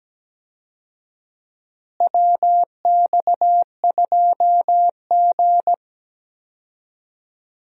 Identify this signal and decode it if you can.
Morse code, heard as a keyed tone, about 17 wpm, 700 Hz. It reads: WX2G
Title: USA Call Signs - 4 Characters - Repeated 2x - 17wpm